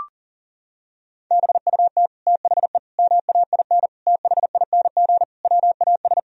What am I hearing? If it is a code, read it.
Morse code, heard as a keyed tone, about 37 wpm, 700 Hz. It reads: BUT THE MAIN THING WAS
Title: Sentences from Top 1000 Words - Review Part 2 - 40wpm